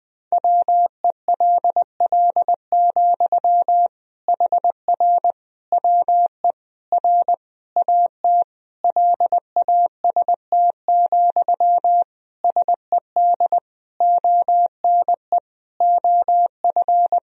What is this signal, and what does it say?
Morse code, heard as a keyed tone, about 20 wpm, 700 Hz. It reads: WELL, HR WE R AT LAST, SED ONE OF
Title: Sentences from Top 300 Words and Abbreviations 20wpm